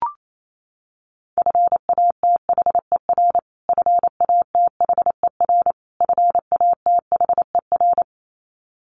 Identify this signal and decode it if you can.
Morse code, heard as a keyed tone, about 28 wpm, 700 Hz. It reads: FATHER FATHER FATHER
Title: Top 200 Words Repeated 28wpm